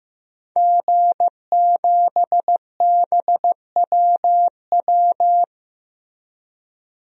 Morse code, heard as a keyed tone, about 15 wpm, 700 Hz. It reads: G7BWW
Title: DXCC-100 Call Signs - 5 Characters - Repeated 2x - 15wpm